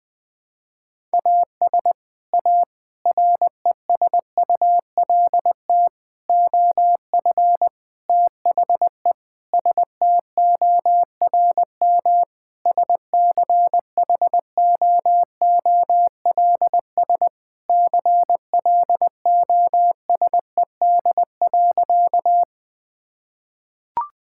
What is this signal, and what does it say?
Morse code, heard as a keyed tone, about 20 wpm, 700 Hz. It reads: AS A RESULT OF THE STORM SCHOOLS CLOSED.
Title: Sentences with 5-Word Phrases - 20wpm